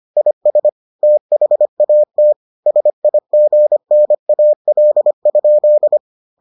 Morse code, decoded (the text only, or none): IS THAT SIGNAL?